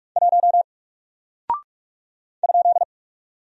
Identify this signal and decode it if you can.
Morse code, heard as a keyed tone, about 44 wpm, 700 Hz. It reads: 1 ?
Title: Single Character (Letter:Number:Punctuation:Prosign) Alt 45wpm